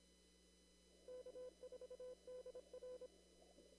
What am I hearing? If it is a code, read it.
Morse code, heard as a keyed tone, about 26 wpm, 500 Hz. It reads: K4DR
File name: CW / kt5j wpx cw-op-k5pi